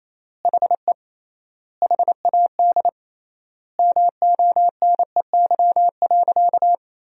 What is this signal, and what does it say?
Morse code, heard as a keyed tone, about 28 wpm, 700 Hz. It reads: HE HAD MONEY.
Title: 3x Word-Spacing - Sentences from Top 500 Words - 28wpm